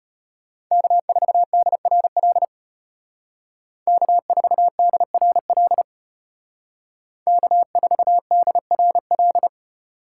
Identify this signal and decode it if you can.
Morse code, heard as a keyed tone, about 32 wpm, 700 Hz. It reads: K4DRL K4DRL K4DRL